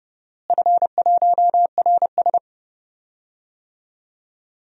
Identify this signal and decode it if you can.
Morse code, heard as a keyed tone, about 30 wpm, 700 Hz. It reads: F1RS
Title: DXCC-100 Call Signs - 4 Characters - Repeated 2x - 30wpm